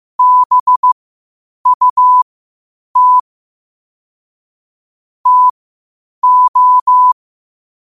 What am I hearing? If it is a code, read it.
Morse code, heard as a keyed tone, about 15 wpm, spaced as for 8 wpm, 1000 Hz. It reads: BUT TO
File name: qotdfiles / 8wpm